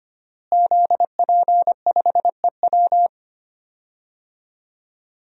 Morse code, heard as a keyed tone, about 25 wpm, 700 Hz. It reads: ZP5EW
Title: DXCC-100 Call Signs - 5 Characters - Repeated 2x - 25wpm